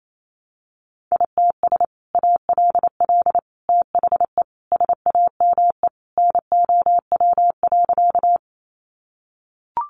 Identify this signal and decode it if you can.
Morse code, heard as a keyed tone, about 28 wpm, 700 Hz. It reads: ITS ALL THE SAME NOW.
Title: Sentences from Top 600 Words 28wpm